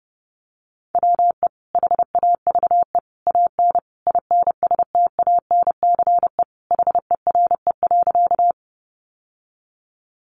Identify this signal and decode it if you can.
Morse code, heard as a keyed tone, about 28 wpm, 700 Hz. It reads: WE HAVE AN INSTANCE HERE.
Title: Sentences from Top 800 Words 30wpm